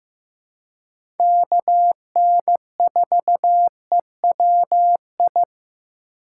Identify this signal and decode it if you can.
Morse code, heard as a keyed tone, about 15 wpm, 700 Hz. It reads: KN4EWI
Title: K1USN SST QSO Practice - 15wpm